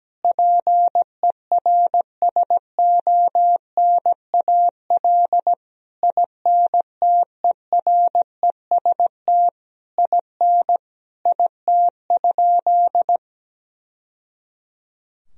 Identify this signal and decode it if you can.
Morse code, heard as a keyed tone, about 17 wpm, 700 Hz. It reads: PERSONAL INTEREST IN IT?
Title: Sentences from Top 1000 Words - Review Part 4 - 17wpm